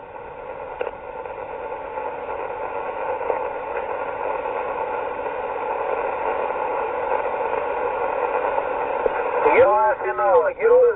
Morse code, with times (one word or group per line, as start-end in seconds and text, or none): none